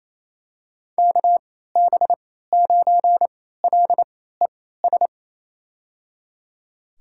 Morse code, heard as a keyed tone, about 28 wpm, spaced as for 16 wpm, 700 Hz. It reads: KB9LES
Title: USA Call Signs - 3x Character Spacing - 28wpm